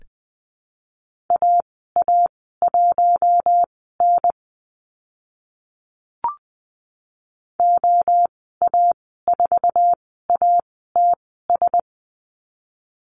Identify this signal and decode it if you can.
Morse code, heard as a keyed tone, about 20 wpm, spaced as for 14 wpm, 700 Hz. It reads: AA1N OA4ATS
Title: 062 Lesson 11g - Letter h - Call Signs 20wpm